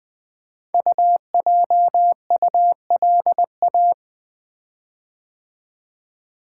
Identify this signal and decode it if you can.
Morse code, heard as a keyed tone, about 21 wpm, 700 Hz. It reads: UJULA
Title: Code Groups - 5 Letters - 20wpm